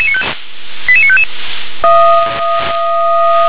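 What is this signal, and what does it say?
Five-tone selective call (ZVEI), heard as quick runs of five tones, tones of about 70 ms. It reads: 88955
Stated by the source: ZVEI five-tone selective call 88953, received on SDR.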